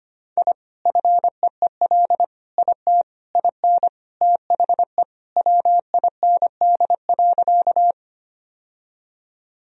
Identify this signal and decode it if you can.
Morse code, heard as a keyed tone, about 25 wpm, 700 Hz. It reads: I FEEL IT IN THE WIND.
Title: Sentences from Top 1000 Words - Review Part 2 - 25wpm